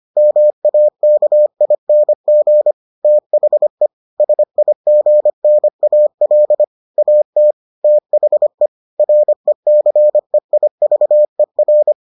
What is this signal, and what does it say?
Morse code, heard as a keyed tone, about 25 wpm, 600 Hz. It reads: MAKING THE SIGNAL AT THE RECEIVER